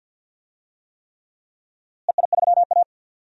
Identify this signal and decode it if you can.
Morse code, heard as a keyed tone, about 48 wpm, 700 Hz. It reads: EI3A
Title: DXCC-100 Call Signs - 4 Characters - Repeated 3x - 50wpm